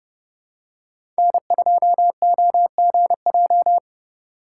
Morse code, heard as a keyed tone, about 31 wpm, 700 Hz. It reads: N2OGJ
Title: USA Call Signs - 5 Characters - 4x Character Spacing - 30wpm